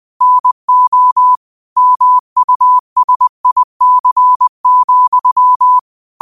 Morse code, heard as a keyed tone, about 20 wpm, 1000 Hz. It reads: NO MUSIC,
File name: qotdfiles / 20wpm